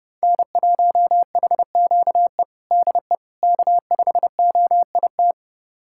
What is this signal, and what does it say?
Morse code, heard as a keyed tone, about 29 wpm, 700 Hz. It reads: N1HQE DE K5OIT
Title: USA Callsign DE Callsign - 5 Characters - 30wpm